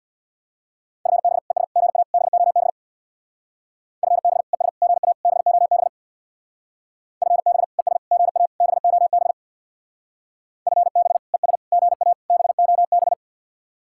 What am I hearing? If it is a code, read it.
Morse code, heard as a keyed tone, about 63 wpm, 700 Hz. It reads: FB ES GA BOB FB ES GA BOB FB ES GA BOB FB ES GA BOB